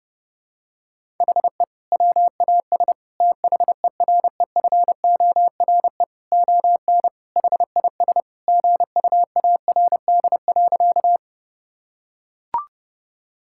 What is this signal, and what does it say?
Morse code, heard as a keyed tone, about 32 wpm, 700 Hz. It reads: HE WAS THEREFORE ON HIS GUARD.
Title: Sentences from Top 1000 Words - Review Part 5 - 30wpm